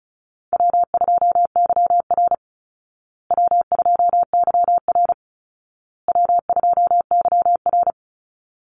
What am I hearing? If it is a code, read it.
Morse code, heard as a keyed tone, about 34 wpm, 700 Hz. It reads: W2YR W2YR W2YR
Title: Easy Call Signs Repeated 3x - 35wpm